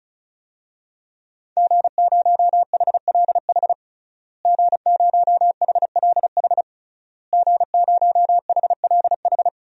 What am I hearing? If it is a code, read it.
Morse code, heard as a keyed tone, about 36 wpm, 700 Hz. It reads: G0HLH G0HLH G0HLH
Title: DXCC-100 Call Signs - 5 Characters - Repeated 3x - 35wpm